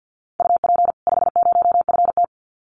Morse code, heard as a keyed tone, about 50 wpm, 700 Hz. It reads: UP 50FT
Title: Common QSO Phrases 50wpm